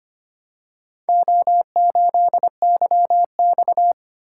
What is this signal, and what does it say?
Morse code, heard as a keyed tone, about 25 wpm, 700 Hz. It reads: O8YX
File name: practice / 25wpm / Four Letter-Number Combinations - 25wpm